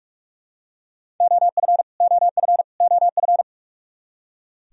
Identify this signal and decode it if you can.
Morse code, heard as a keyed tone, about 46 wpm, 700 Hz. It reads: OF OF OF
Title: Top 100 Words Repeated 45wpm